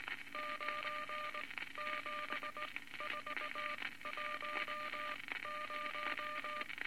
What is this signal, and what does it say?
Morse code, heard as a keyed tone, about 19 wpm, spaced as for 15 wpm, 1200 Hz, with noise. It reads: 97410
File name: Morse Code Records / 19 B LCM 167 Twelve Words Per Minute